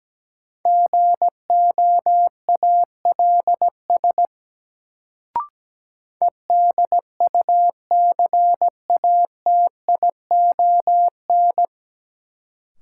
Morse code, heard as a keyed tone, about 17 wpm, 700 Hz. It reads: GOALS EDUCATION
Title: Sentences from Top 1000 Words - Review Part 6 - New Word Drill - 17wpm